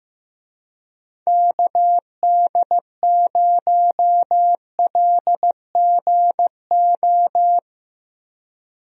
Morse code, heard as a keyed tone, about 15 wpm, 700 Hz. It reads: KD0LGO
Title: USA Call Signs - 6 Characters - 15wpm